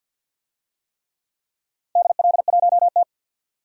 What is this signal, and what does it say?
Morse code, heard as a keyed tone, about 50 wpm, 700 Hz. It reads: DL1T
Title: DXCC-100 Call Signs - 4 Characters - 50wpm